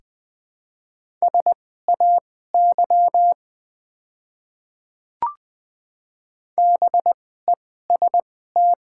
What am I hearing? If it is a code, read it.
Morse code, heard as a keyed tone, about 20 wpm, spaced as for 14 wpm, 700 Hz. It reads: SAY BEST
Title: 2x Character Spacing - Top 500 Words - 20wpm